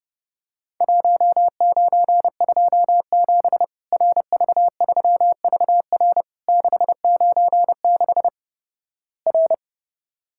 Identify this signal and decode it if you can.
Morse code, heard as a keyed tone, about 30 wpm, 700 Hz. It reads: 1927 RV3VR 696 R
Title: Worked All Europe DX Contest - QTC Exchange - 30wpm